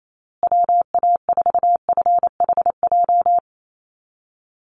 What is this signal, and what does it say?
Morse code, heard as a keyed tone, about 28 wpm, 700 Hz. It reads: WA4FHJ
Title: USA Call Signs Repeated 2x 28wpm